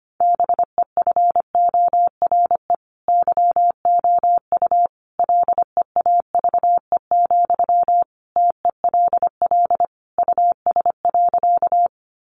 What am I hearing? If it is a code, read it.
Morse code, heard as a keyed tone, about 25 wpm, 700 Hz. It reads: BEFORE YOU LEAVE, TELL US.
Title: Sentences from Top 400 Words 25wpm